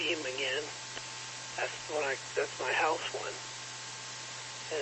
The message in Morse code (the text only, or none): none